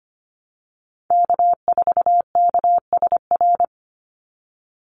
Morse code, heard as a keyed tone, about 25 wpm, 700 Hz. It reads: K4KSR
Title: Call Signs Repeated 2x - 25wpm